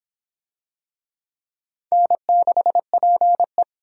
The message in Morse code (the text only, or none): N6PE